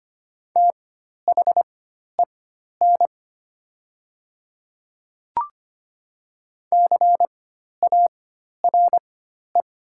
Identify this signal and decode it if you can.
Morse code, heard as a keyed tone, about 25 wpm, spaced as for 12 wpm, 690 Hz. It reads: THEN CARE
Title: 4x Character Spacing - Top 300 Words - 25wpm